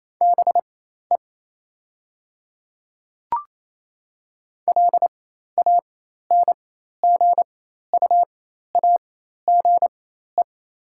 Morse code, heard as a keyed tone, about 28 wpm, spaced as for 13 wpm, 700 Hz. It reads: BE LANGUAGE
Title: 4x Character Spacing - Top 500 Words - 28wpm